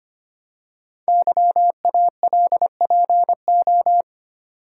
Morse code, heard as a keyed tone, about 25 wpm, 700 Hz. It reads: YALPO
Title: Code Groups - 5 Letters - 25wpm